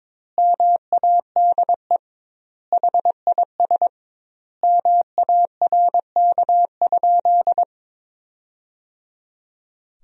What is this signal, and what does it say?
Morse code, heard as a keyed tone, about 22 wpm, 700 Hz. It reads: MADE HIS MARK?
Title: 2x Word-Spacing - Sentences from Top 300 Words - 22wpm